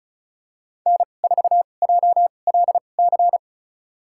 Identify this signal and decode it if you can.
Morse code, heard as a keyed tone, about 36 wpm, spaced as for 25 wpm, 700 Hz. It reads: N4JLC
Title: USA Call Signs - 5 Characters - 2x Character Spacing - 35wpm